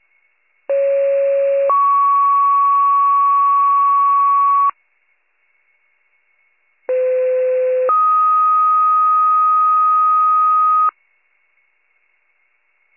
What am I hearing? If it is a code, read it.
Two-tone page, heard as a one-second tone, then a three-second tone, about 550 then 1120 Hz, about 510 then 1250 Hz. It is heard from 0.7 to 10.9 s.